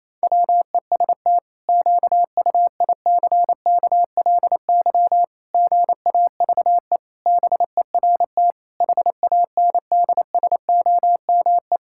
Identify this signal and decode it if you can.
Morse code, heard as a keyed tone, about 28 wpm, 700 Hz. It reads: WEST QUICKLY GAVE BERT HANDSOME
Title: Pangram Challenge v2 - 28wpm